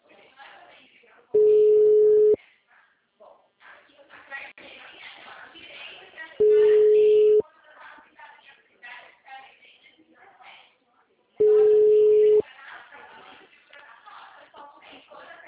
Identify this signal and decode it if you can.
Ringback tone (Europe, 425 Hz), with speech.